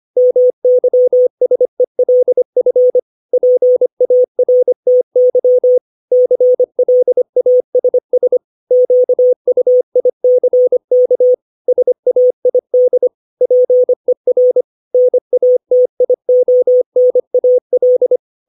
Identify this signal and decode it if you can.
Morse code, heard as a keyed tone, about 25 wpm, 500 Hz. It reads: MYSELF PARTY CLASS QUICK SAID PER NATIONAL
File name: 632 words / 632 Words 25WPM